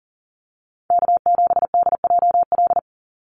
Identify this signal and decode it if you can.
Morse code, heard as a keyed tone, about 40 wpm, 700 Hz. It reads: K7DJL
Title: NA SOTA Chaser Callsigns 40wpm